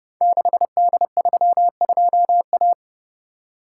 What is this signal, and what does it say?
Morse code, heard as a keyed tone, about 30 wpm, 700 Hz. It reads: 6D32A
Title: Code Groups - 5 Letter:Numbers - 30wpm